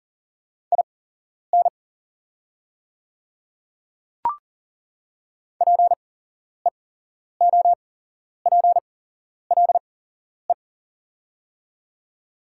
Morse code, heard as a keyed tone, about 40 wpm, spaced as for 11 wpm, 700 Hz. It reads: IN PEOPLE
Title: 8x Character Spacing - Top 300 Words - 40wpm